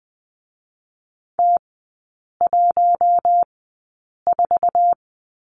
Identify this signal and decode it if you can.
Morse code, heard as a keyed tone, about 21 wpm, spaced as for 8 wpm, 700 Hz. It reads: T14
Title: 047 Lesson 9e - Number 4 - Three Characters 20wpm